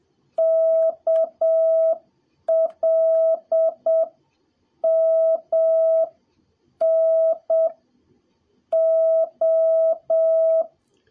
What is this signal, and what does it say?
Morse code, heard as a keyed tone, about 7 wpm, spaced as for 6 wpm, 650 Hz. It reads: KLMN O
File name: jh8cbh / 2021cw / cw03a